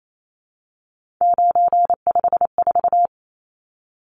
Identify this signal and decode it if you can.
Morse code, heard as a keyed tone, about 28 wpm, 700 Hz. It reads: 954
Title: Numbers - 3-Digit 28wpm